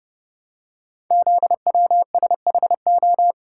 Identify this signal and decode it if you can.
Morse code, heard as a keyed tone, about 30 wpm, 700 Hz. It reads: ZWSHO
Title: Code Groups - 5 Letters - 30wpm